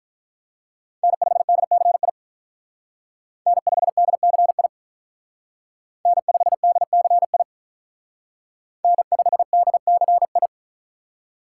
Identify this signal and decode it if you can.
Morse code, heard as a keyed tone, about 43 wpm, 700 Hz. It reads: N5DCI N5DCI N5DCI N5DCI
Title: Speed Racing - USA 1x3 Call Signs 35wpm